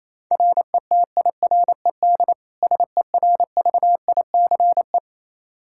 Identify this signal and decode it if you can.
Morse code, heard as a keyed tone, about 28 wpm, 700 Hz. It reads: RETIRED SERVICE
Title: Practice QSO - Exchange 3 - More Info - 28wpm